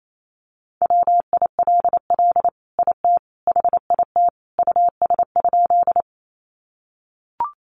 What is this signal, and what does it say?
Morse code, heard as a keyed tone, about 28 wpm, 700 Hz. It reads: WILL IT HIT US?
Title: Sentences from Top 1200 Words 28wpm